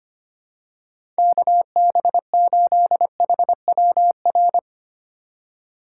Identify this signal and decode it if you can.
Morse code, heard as a keyed tone, about 25 wpm, 700 Hz. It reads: KB8HWR
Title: USA Call Signs - 6 Characters - 2x Character Spacing - 25wpm